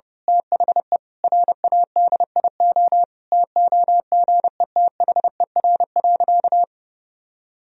Morse code, heard as a keyed tone, about 32 wpm, 700 Hz. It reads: THE RADIO TOGETHER.
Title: Sentences from Top 1000 Words - Review Part 3 - 30wpm